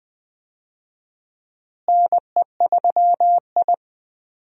Morse code, heard as a keyed tone, about 20 wpm, 700 Hz. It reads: NE3I
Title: USA Call Signs - 4 Characters - Repeated 3x - 20wpm